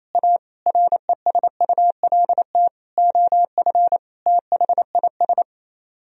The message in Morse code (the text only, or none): A RESULT OF THIS